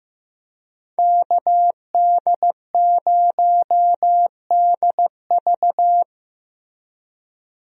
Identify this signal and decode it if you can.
Morse code, heard as a keyed tone, about 15 wpm, 700 Hz. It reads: KD0DV
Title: USA Call Signs - 5 Characters - Repeated 3x - 15wpm